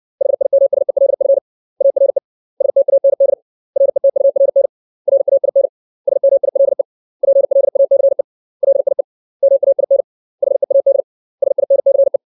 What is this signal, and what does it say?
Morse code, heard as a keyed tone, about 60 wpm, 550 Hz. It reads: HIMSELF ARE STAND RETURN RAIN SMILE PLACE LIE MAIN HEAD SINCE
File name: 632 words / 632 Words 60WPM